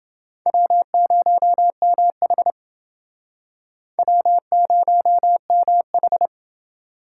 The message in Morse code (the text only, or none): W0MH W0MH